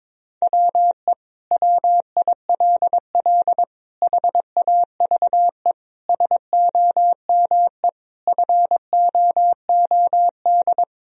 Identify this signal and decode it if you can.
Morse code, heard as a keyed tone, about 22 wpm, 700 Hz. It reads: WE WILL HAVE SOME FOOD
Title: Sentences from Top 1000 Words - Review Part 1 - 22wpm